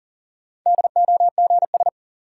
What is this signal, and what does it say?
Morse code, heard as a keyed tone, about 40 wpm, 700 Hz. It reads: DOGS
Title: Sentences from Top 1600 Words - New Word Drill Alt - 40wpm